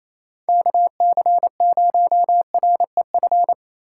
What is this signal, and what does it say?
Morse code, heard as a keyed tone, about 34 wpm, 700 Hz. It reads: KC0REF
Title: Speed Racing - USA 2x3 Call Signs 28wpm